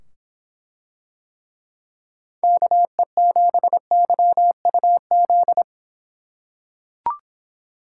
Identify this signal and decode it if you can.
Morse code, heard as a keyed tone, about 24 wpm, 700 Hz. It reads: KE7YUZ